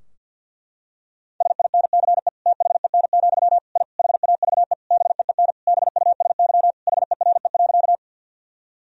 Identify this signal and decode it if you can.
Morse code, heard as a keyed tone, about 47 wpm, 700 Hz. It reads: SINCE THEN, I HAVE BEEN BUSY HERE.
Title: Sentences from Top 1000 Words - Review Part 3 - 50wpm